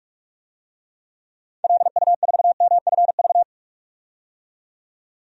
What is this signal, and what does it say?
Morse code, heard as a keyed tone, about 44 wpm, 700 Hz. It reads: LU4MFV
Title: DXCC-100 Call Signs - 6 Characters - 45wpm